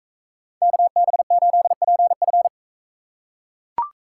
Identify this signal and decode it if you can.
Morse code, heard as a keyed tone, about 40 wpm, 700 Hz. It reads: KB8PF